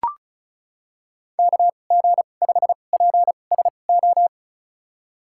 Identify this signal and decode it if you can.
Morse code, heard as a keyed tone, about 34 wpm, spaced as for 25 wpm, 700 Hz. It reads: KG5PSO
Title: USA Call Signs - 6 Characters - 2x Character Spacing - 35wpm